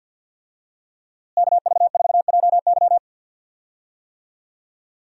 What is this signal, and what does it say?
Morse code, heard as a keyed tone, about 47 wpm, 700 Hz. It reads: KV4JY
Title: USA Call Signs - 5 Characters - Repeated 3x - 50wpm